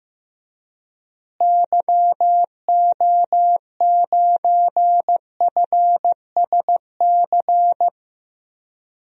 Morse code, heard as a keyed tone, about 15 wpm, 700 Hz. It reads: YO9FSC
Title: DXCC-100 Call Signs - 6 Characters - Repeated 2x - 15wpm